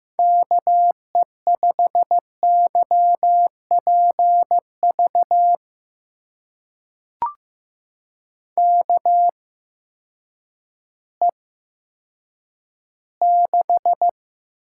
Morse code, heard as a keyed tone, about 15 wpm, spaced as for 9 wpm, 700 Hz. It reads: KE5YPV KE6
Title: USA Call Signs - 6 Characters - 8x Character Spacing - 15wpm